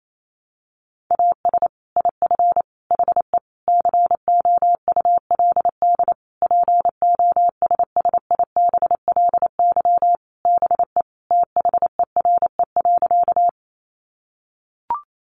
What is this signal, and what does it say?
Morse code, heard as a keyed tone, about 28 wpm, 700 Hz. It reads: AS IF HE COULD POSSIBLY BE THERE.
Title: Sentences from Top 1300 Words 28wpm